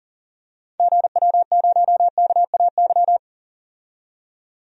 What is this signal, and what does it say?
Morse code, heard as a keyed tone, about 40 wpm, 700 Hz. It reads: GW0KAY
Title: DXCC-100 Call Signs - 6 Characters - 40wpm